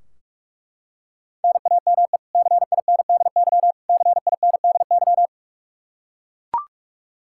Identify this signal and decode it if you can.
Morse code, heard as a keyed tone, about 41 wpm, 700 Hz. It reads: NAME CINDY CINDY
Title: K1USN SST QSO Practice 45wpm